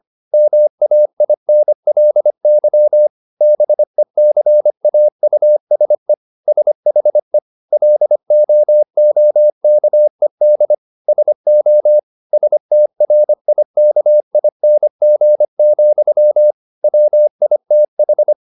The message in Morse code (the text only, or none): MAINLY BECAUSE SHE LOOKED SO STRIKING, WITH